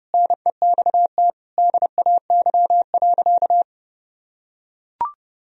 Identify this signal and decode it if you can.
Morse code, heard as a keyed tone, about 32 wpm, 700 Hz. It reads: NEXT DAY.
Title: Sentences from Top 1000 Words - Review Part 1 - 30wpm